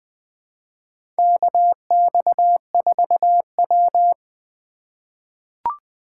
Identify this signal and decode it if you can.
Morse code, heard as a keyed tone, about 20 wpm, 700 Hz. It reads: KX4W
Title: USA Call Signs - 4 Characters - 2x Character Spacing - 20wpm